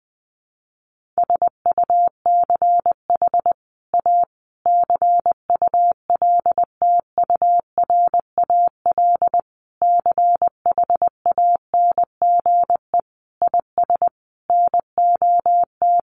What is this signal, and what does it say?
Morse code, heard as a keyed tone, about 20 wpm, 700 Hz. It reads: SUCH A CULTURAL CHANGE IS NOT
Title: Sentences from Top 1500 Words 20wpm